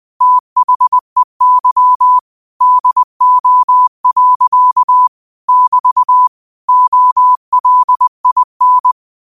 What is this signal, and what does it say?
Morse code, heard as a keyed tone, about 20 wpm, 1000 Hz. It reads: THEY DO. <BT> OLIN
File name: qotdfiles / 20wpm